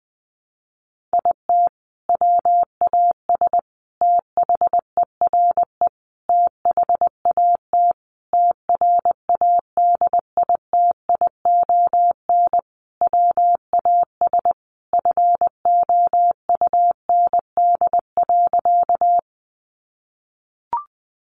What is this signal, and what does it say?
Morse code, heard as a keyed tone, about 20 wpm, 700 Hz. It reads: IT WAS THERE THAT TRADITION WAS FOUND.
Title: Sentences from Top 1500 Words 20wpm